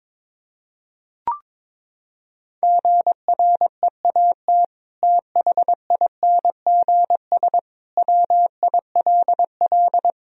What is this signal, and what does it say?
Morse code, heard as a keyed tone, about 22 wpm, 700 Hz. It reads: GREAT THINGS WILL
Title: Sentences from Top 1000 Words - Review Part 4 - 22wpm